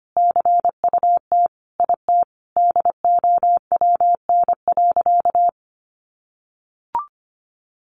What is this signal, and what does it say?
Morse code, heard as a keyed tone, about 25 wpm, 700 Hz. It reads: CUT IT DOWN.
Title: Sentences from Top 1200 Words 25wpm